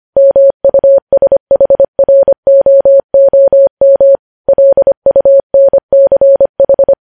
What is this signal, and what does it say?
Morse code, heard as a keyed tone, about 25 wpm, 550 Hz. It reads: MUSHROOM LUNCH
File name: podcast / News headlines-1x-25x25